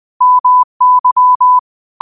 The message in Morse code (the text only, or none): MY